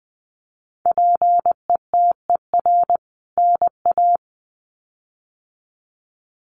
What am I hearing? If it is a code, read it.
Morse code, heard as a keyed tone, about 20 wpm, 700 Hz. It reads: PETER NA
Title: CWT Exchange 20wpm